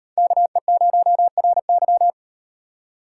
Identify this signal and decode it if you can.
Morse code, heard as a keyed tone, about 38 wpm, 700 Hz. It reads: KE0RY